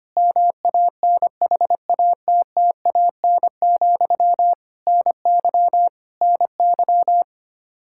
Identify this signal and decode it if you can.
Morse code, heard as a keyed tone, about 25 wpm, 700 Hz. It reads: MANHATTAN, NY NY
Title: Practice QSO - Exchange 1 - Beginning - 25wpm